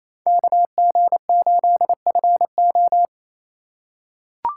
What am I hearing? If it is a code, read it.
Morse code, heard as a keyed tone, about 28 wpm, 700 Hz. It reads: KG8FO
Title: USA Call Signs - 5 Characters - 2x Character Spacing - 28wpm